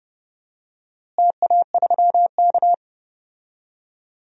Morse code, heard as a keyed tone, about 30 wpm, 700 Hz. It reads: TA3K
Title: DXCC-100 Call Signs - 4 Characters - Repeated 3x - 30wpm